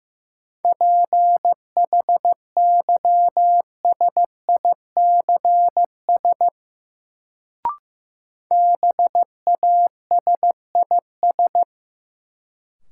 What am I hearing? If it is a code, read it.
Morse code, heard as a keyed tone, about 15 wpm, 700 Hz. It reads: PHYSICS BASIS
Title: Sentences from Top 1000 Words - Review Part 5 - New Word Drill - 15wpm